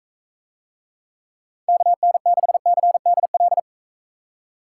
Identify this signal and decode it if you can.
Morse code, heard as a keyed tone, about 40 wpm, 700 Hz. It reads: KN6CDL